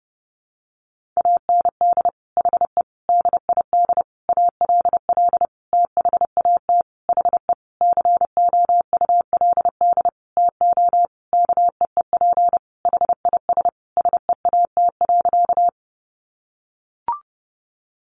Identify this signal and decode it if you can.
Morse code, heard as a keyed tone, about 30 wpm, 700 Hz. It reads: AND HE DID ALL THAT HE COULD TO KEEP HIS SEAT.
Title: Sentences from Top 1500 Words 30wpm